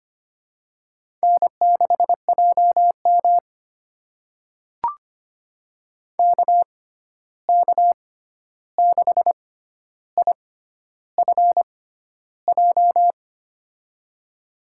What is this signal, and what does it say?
Morse code, heard as a keyed tone, about 25 wpm, spaced as for 9 wpm, 700 Hz. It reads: N6JMK K6IFJ
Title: USA Call Signs - 6x Character Spacing - 25wpm